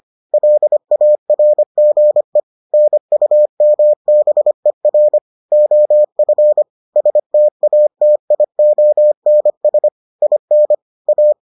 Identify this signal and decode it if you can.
Morse code, heard as a keyed tone, about 25 wpm, 600 Hz. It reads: LARGE NUMBER OF STATIONS IN A